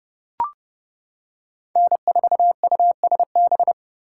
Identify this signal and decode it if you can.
Morse code, heard as a keyed tone, about 29 wpm, 700 Hz. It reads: N4USB